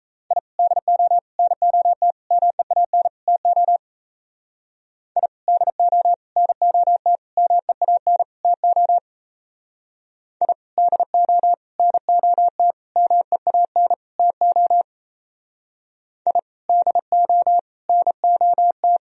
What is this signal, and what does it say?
Morse code, heard as a keyed tone, about 36 wpm, 700 Hz. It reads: I DO NOT MEAN TO I DO NOT MEAN TO I DO NOT MEAN TO I DO NOT